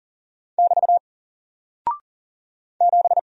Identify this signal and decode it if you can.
Morse code, heard as a keyed tone, about 39 wpm, 700 Hz. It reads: <BT> 7
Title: Single Character (Letter:Number:Punctuation:Prosign) Alt 40wpm